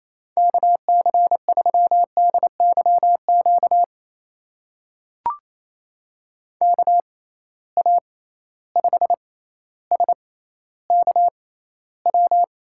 Morse code, heard as a keyed tone, about 28 wpm, spaced as for 14 wpm, 700 Hz. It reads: KC3DYQ KA5SKW
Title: USA Call Signs - 6 Characters - 6x Character Spacing - 28wpm